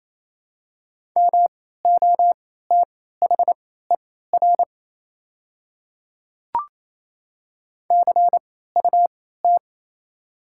Morse code, heard as a keyed tone, about 27 wpm, spaced as for 16 wpm, 700 Hz. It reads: MOTHER CUT
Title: 3x Character Spacing - Top 300 Words - 28wpm